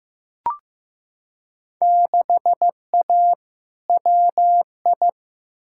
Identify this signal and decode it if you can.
Morse code, heard as a keyed tone, about 15 wpm, 700 Hz. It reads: 6A WI
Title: Field Day Exchanges - 15wpm